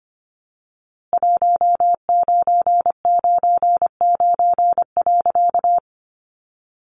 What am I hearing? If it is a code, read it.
Morse code, heard as a keyed tone, about 25 wpm, 700 Hz. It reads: 1999.
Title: A Sound Of Thunder 25wpm